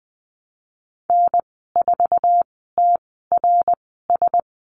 Morse code, heard as a keyed tone, about 20 wpm, spaced as for 15 wpm, 700 Hz. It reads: N4TRS
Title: 069 Lesson 12g - Letter d - Call Signs 20wpm